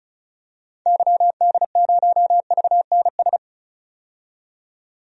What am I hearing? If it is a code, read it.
Morse code, heard as a keyed tone, about 34 wpm, 700 Hz. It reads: YD0VNS
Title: DXCC-100 Call Signs - 6 Characters - 35wpm